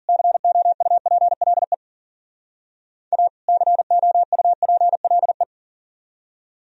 Morse code, heard as a keyed tone, about 49 wpm, 700 Hz. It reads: COUPLE A COUPLE
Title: Speed-Racing - Sets of 2 Words 40wpm